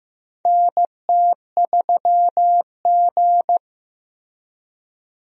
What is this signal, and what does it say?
Morse code, heard as a keyed tone, about 15 wpm, 700 Hz. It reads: NT3G
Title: USA Call Signs - 4 Characters - Repeated 2x - 15wpm